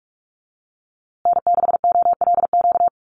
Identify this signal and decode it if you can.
Morse code, heard as a keyed tone, about 44 wpm, 700 Hz. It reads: N6OLQ
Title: USA Call Signs Repeated 3x 45wpm